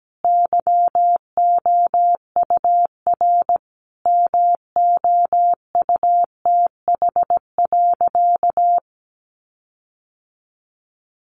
Morse code, heard as a keyed tone, about 17 wpm, 700 Hz. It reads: YOUR MOUTH.
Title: Sentences from Top 1500 Words 17wpm